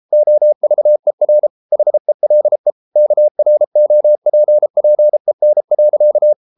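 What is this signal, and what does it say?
Morse code, heard as a keyed tone, about 33 wpm, 600 Hz. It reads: OVER HELE KROPPEN.
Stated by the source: Morse code at 33 wpm, QRQ.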